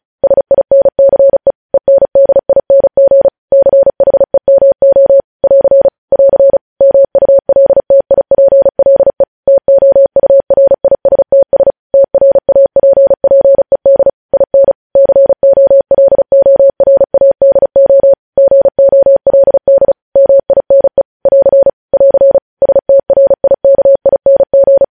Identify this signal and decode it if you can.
Morse code, heard as a keyed tone, about 36 wpm, 550 Hz. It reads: SINCE ENDING CHEMO <AR> <AR> MULTIPLE TOURISTS TRAPPED IN COLORADO GOLD MINE <AR> <AR> STRIKING